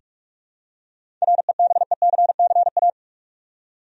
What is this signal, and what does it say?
Morse code, heard as a keyed tone, about 46 wpm, 700 Hz. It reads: REBECCA
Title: USA Common Names - 45wpm